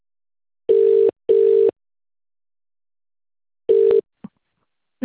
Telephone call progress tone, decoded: ringback tone